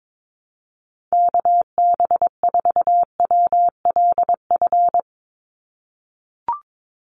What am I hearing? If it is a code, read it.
Morse code, heard as a keyed tone, about 22 wpm, 700 Hz. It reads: KB4WLF
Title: Call Signs - 22wpm